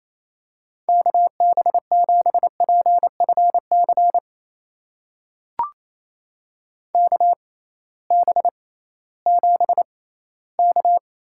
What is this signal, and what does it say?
Morse code, heard as a keyed tone, about 29 wpm, spaced as for 17 wpm, 700 Hz. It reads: KB7PFC KB7K
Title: USA Call Signs - 6 Characters - 6x Character Spacing - 28wpm